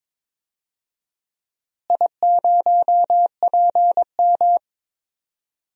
Morse code, heard as a keyed tone, about 22 wpm, 700 Hz. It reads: I0PM
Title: DXCC-100 Call Signs - 4 Characters - Repeated 2x - 22wpm